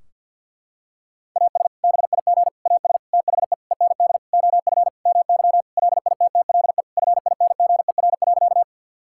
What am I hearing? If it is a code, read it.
Morse code, heard as a keyed tone, about 50 wpm, 700 Hz. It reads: AS BIG AS THE END OF MY LITTLE FINGER.
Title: Sentences from Top 1600 Words 50wpm